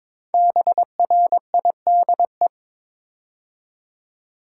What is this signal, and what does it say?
Morse code, heard as a keyed tone, about 22 wpm, 700 Hz. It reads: BRIDE
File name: 5 Letter Words - First 3 Letters Given - 22wpm